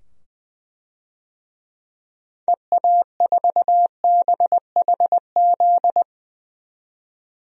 Morse code, heard as a keyed tone, about 20 wpm, 700 Hz. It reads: EA4BHZ
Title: DXCC-100 Call Signs - 6 Characters - 20wpm